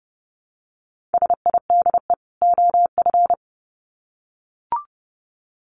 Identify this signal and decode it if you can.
Morse code, heard as a keyed tone, about 30 wpm, 700 Hz. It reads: SIDE OF
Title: Sets of 2 Words 30wpm